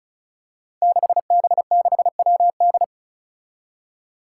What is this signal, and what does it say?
Morse code, heard as a keyed tone, about 36 wpm, 700 Hz. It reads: 6B6WD
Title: Five Challenging Letter-Number Combinations 35wpm